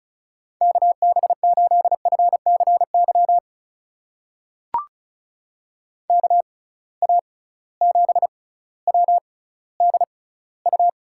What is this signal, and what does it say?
Morse code, heard as a keyed tone, about 36 wpm, spaced as for 18 wpm, 700 Hz. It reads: KB8FCY KA7WDU
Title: USA Call Signs - 6 Characters - 6x Character Spacing - 35wpm